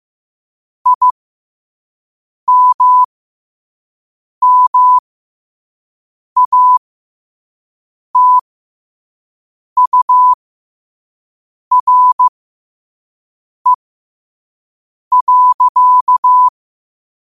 Morse code, heard as a keyed tone, about 15 wpm, spaced as for 5 wpm, 1000 Hz. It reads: IMMATURE.
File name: qotdfiles / 5wpm